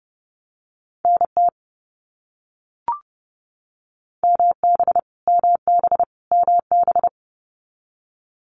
Morse code, heard as a keyed tone, about 30 wpm, 700 Hz. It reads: NT MB MB MB
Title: Canadian Provinces and Territories - Repeated 3x - 30wpm